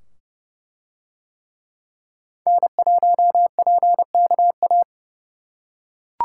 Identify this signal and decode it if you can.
Morse code, heard as a keyed tone, about 28 wpm, 700 Hz. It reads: N1PKA